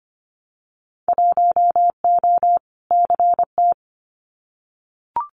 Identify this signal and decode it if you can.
Morse code, heard as a keyed tone, about 25 wpm, 700 Hz. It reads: 1O CT
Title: Winter Field Day Exchanges 25wpm